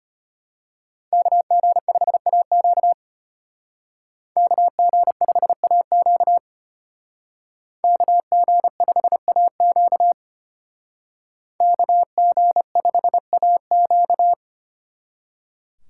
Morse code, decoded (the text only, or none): KG5AQ KG5AQ KG5AQ KG5AQ